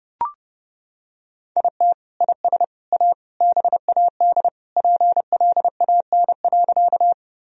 Morse code, heard as a keyed tone, about 30 wpm, 700 Hz. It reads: IT IS A BAD PLAN.
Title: Sentences from Top 1000 Words - Review Part 2 - 30wpm